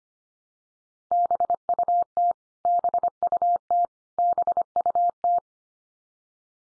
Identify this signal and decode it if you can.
Morse code, heard as a keyed tone, about 25 wpm, 700 Hz. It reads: BUT BUT BUT
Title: Top 100 Words 25wpm x3